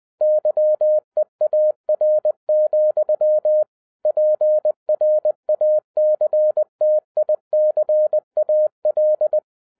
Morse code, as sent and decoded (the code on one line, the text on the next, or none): -.-- . .- .-. --..-- / .--. .-. .- -.-. - .. -.-. .- .-..
YEAR, PRACTICAL